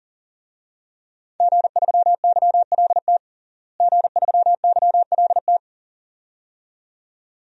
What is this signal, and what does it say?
Morse code, heard as a keyed tone, about 37 wpm, 700 Hz. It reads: G3YLT G3YLT
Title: DXCC-100 Call Signs - 5 Characters - Repeated 2x - 40wpm